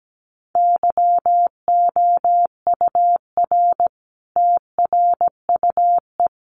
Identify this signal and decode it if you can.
Morse code, heard as a keyed tone, about 17 wpm, 700 Hz. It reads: YOUR TRUE
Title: Sentences from Top 400 Words 17wpm